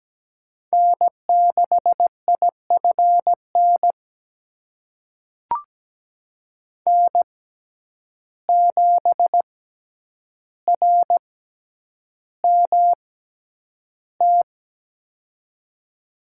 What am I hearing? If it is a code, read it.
Morse code, heard as a keyed tone, about 17 wpm, spaced as for 9 wpm, 700 Hz. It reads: N6IFN N7RMT
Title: USA Call Signs - 5 Characters - 6x Character Spacing - 17wpm